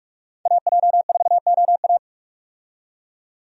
Morse code, heard as a keyed tone, about 42 wpm, 700 Hz. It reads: AJ4OA
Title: USA Call Signs - 5 Characters - 45wpm